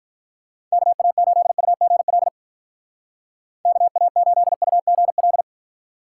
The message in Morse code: KA8UGL KA8UGL

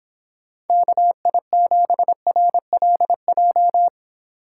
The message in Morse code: KI7RLJ

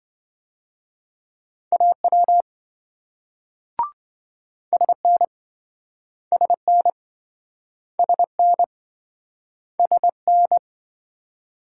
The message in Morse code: AW SN SN SN SN

